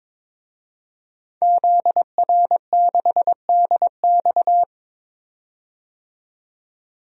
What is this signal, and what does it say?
Morse code, heard as a keyed tone, about 22 wpm, 700 Hz. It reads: ZR6DX
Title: DXCC-100 Call Signs - 5 Characters - 22wpm